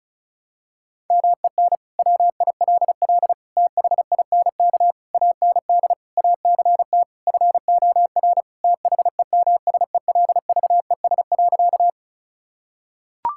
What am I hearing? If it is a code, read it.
Morse code, heard as a keyed tone, about 36 wpm, 700 Hz. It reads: MEN WILL THINK AND ACT FOR THEMSELVES.
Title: Sentences from Top 1000 Words - Review Part 5 - 35wpm